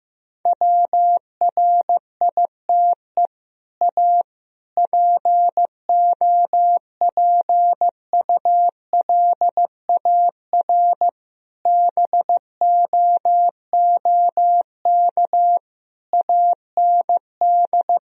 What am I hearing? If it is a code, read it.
Morse code, heard as a keyed tone, about 16 wpm, 700 Hz. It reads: WRITE A POPULAR BOOK AND
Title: Sentences from Top 1000 Words - Review Part 3 - 15wpm